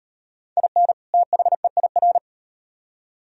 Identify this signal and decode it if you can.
Morse code, heard as a keyed tone, about 34 wpm, 700 Hz. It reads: IN THEIR